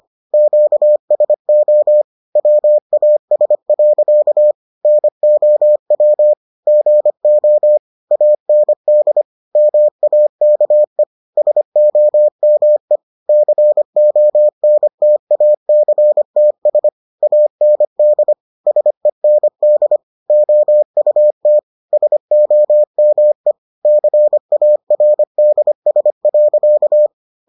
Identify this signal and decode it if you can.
Morse code, heard as a keyed tone, about 25 wpm, 600 Hz. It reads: QSO WAS. NOW GO AND MAKE SOME CONTACTS AND SEND OUT SOME CARDS.